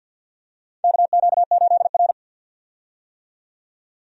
Morse code, heard as a keyed tone, about 51 wpm, 700 Hz. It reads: KQ8R
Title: USA Call Signs - 4 Characters - 50wpm